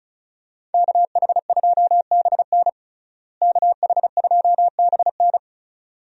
Morse code, heard as a keyed tone, about 34 wpm, 700 Hz. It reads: KH2BN KH2BN
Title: USA Call Signs - 5 Characters - Repeated 2x - 35wpm